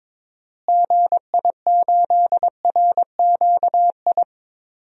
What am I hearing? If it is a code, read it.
Morse code, heard as a keyed tone, about 22 wpm, 700 Hz. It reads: GI8RQI
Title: DXCC-100 Call Signs - 6 Characters - Repeated 2x - 22wpm